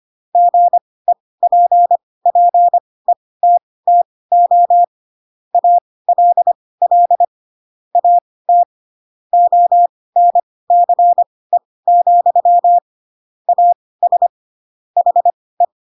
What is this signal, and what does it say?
Morse code, heard as a keyed tone, about 25 wpm, spaced as for 18 wpm, 700 Hz. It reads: GEPPETTO ALL AT ONCE, AS HE